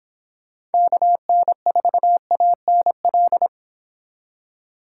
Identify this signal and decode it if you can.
Morse code, heard as a keyed tone, about 31 wpm, 700 Hz. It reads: KN4ANL